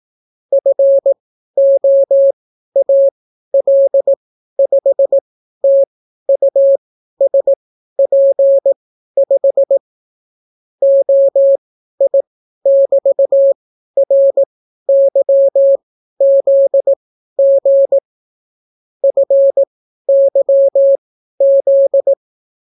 Morse code, heard as a keyed tone, about 18 wpm, spaced as for 12 wpm, 550 Hz. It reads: FOAL5TUSP5 OI<BT>RYZG FYZ